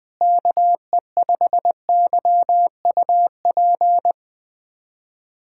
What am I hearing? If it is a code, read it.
Morse code, heard as a keyed tone, about 21 wpm, 700 Hz. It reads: KE5YUP
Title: USA Callsign DE Callsign - 6 Characters - 20wpm